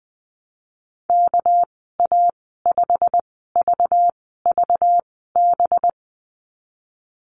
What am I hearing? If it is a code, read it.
Morse code, heard as a keyed tone, about 20 wpm, spaced as for 15 wpm, 700 Hz. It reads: KA5VVB
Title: 213 Lesson 34g - Number 8 - Call Signs 20wpm